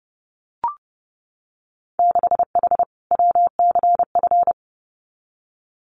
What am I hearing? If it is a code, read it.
Morse code, heard as a keyed tone, about 30 wpm, 700 Hz. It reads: 6H WCF
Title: Winter Field Day Exchanges Encore 30wpm